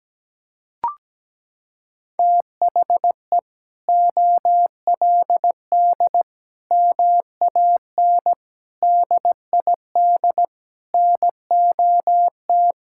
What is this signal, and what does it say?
Morse code, heard as a keyed tone, about 17 wpm, 700 Hz. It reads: THE OLD MAN DID NOT
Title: Sentences from Top 1600 Words 17wpm